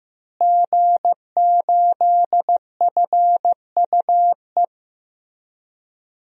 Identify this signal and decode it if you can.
Morse code, heard as a keyed tone, about 15 wpm, 700 Hz. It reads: G8FUE
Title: DXCC-100 Call Signs - 5 Characters - Repeated 3x - 15wpm